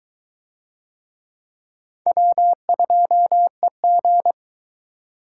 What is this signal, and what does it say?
Morse code, heard as a keyed tone, about 22 wpm, 700 Hz. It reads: W2EG